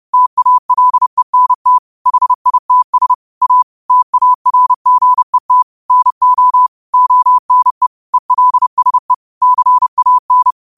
Morse code, heard as a keyed tone, about 30 wpm, 1000 Hz. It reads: TALENT HITS A TARGET NO ONE ELSE CAN